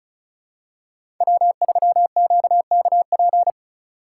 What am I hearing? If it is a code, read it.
Morse code, heard as a keyed tone, about 36 wpm, 700 Hz. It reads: W3QKP
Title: USA Call Signs - 5 Characters - 2x Character Spacing - 35wpm